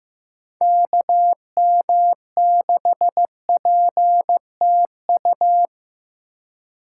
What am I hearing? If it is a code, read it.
Morse code, heard as a keyed tone, about 15 wpm, 700 Hz. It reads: KM6PTU
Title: USA Call Signs - 2x Character Spacing - 15wpm